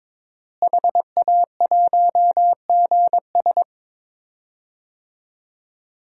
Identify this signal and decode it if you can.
Morse code, heard as a keyed tone, about 22 wpm, 700 Hz. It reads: HA1GS
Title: DXCC-100 Call Signs - 5 Characters - Repeated 2x - 22wpm